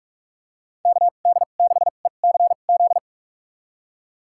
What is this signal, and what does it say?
Morse code, heard as a keyed tone, about 44 wpm, spaced as for 32 wpm, 700 Hz. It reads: KD6ECZ
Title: USA Call Signs - 6 Characters - 2x Character Spacing - 45wpm